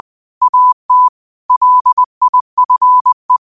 Morse code, heard as a keyed tone, about 20 wpm, 1000 Hz. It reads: AT LIFE
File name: qotdfiles / 20wpm